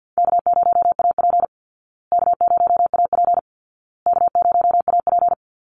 Morse code, heard as a keyed tone, about 50 wpm, 700 Hz. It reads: K0AP K0AP K0AP
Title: Easy Call Signs Repeated 3x - 50wpm